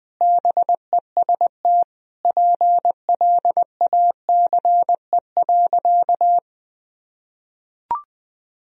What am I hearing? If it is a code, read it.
Morse code, heard as a keyed tone, about 21 wpm, 700 Hz. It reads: BEST PLACE.
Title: Sentences from Top 1000 Words - Review Part 4 - 20wpm